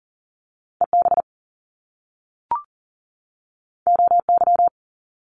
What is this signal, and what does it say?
Morse code, heard as a keyed tone, about 40 wpm, 700 Hz. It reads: EB OY
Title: Top Two Letter Combinations in Words 40wpm